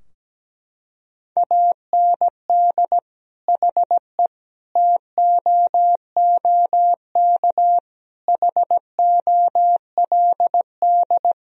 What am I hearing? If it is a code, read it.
Morse code, heard as a keyed tone, about 17 wpm, 700 Hz. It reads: AND HE TOOK HOLD
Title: Sentences from Top 1000 Words - Review Part 4 - 17wpm